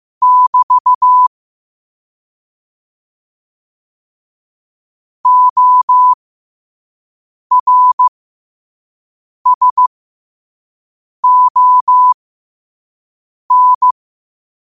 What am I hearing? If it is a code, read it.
Morse code, heard as a keyed tone, about 15 wpm, spaced as for 5 wpm, 1000 Hz. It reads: <BT> ORSON
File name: 5wpm